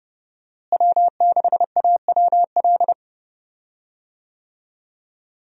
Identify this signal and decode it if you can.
Morse code, heard as a keyed tone, about 32 wpm, 700 Hz. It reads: W6AWL
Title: USA Call Signs - 5 Characters - Repeated 2x - 30wpm